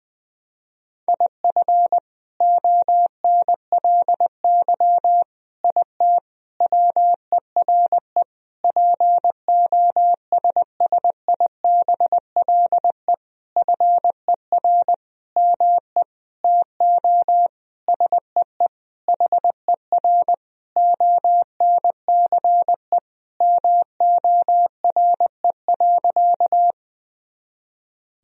Morse code, heard as a keyed tone, about 20 wpm, 700 Hz. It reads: IF ONLY IT WERE POSSIBLE FER ME TO SEE HER ONCE MORE.
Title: Sentences from Top 500 Words and Abbreviations 20wpm